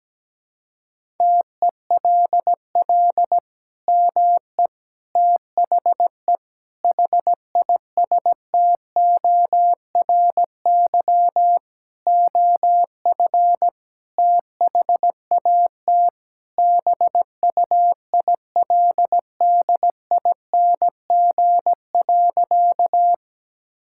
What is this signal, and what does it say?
Morse code, heard as a keyed tone, about 17 wpm, 700 Hz. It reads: TELL ME THE HISTORY OF THAT BUILDING.
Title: Sentences from Top 1000 Words - Review Part 5 - 17wpm